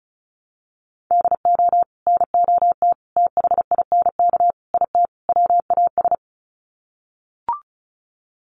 Morse code, heard as a keyed tone, about 34 wpm, 700 Hz. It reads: DO NOT THINK IT WAS
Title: Sets of 5 Words Encore 35wpm